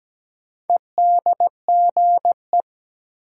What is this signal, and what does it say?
Morse code, heard as a keyed tone, about 17 wpm, 700 Hz. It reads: EDGE
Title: Sets of 3 Words - New Word Drill - 17wpm